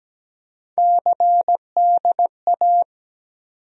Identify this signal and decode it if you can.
Morse code, heard as a keyed tone, about 17 wpm, 700 Hz. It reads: CDA
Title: USA Call Sign Postfixes - 3 Letters 17wpm